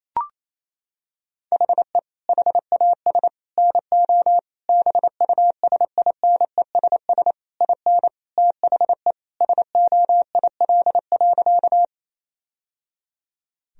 Morse code, heard as a keyed tone, about 28 wpm, 700 Hz. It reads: HE HAS NO BUSINESS IN THE SOIL.
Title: Sentences from Top 1000 Words - Review Part 4 - 28wpm